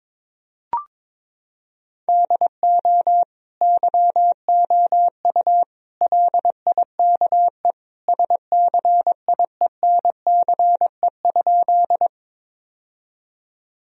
Morse code, heard as a keyed tone, about 22 wpm, 700 Hz. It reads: DO YOU LIKE SCIENCE?
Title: Sentences from Top 1000 Words - Review Part 2 - 22wpm